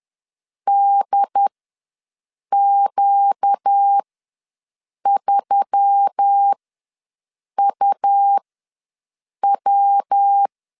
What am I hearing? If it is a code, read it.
Morse code, heard as a keyed tone, about 10 wpm, spaced as for 6 wpm, 800 Hz. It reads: DQ3UW
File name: LEZIONI / 8 YJ45 / 8 YJ45 ws 10wpm 3843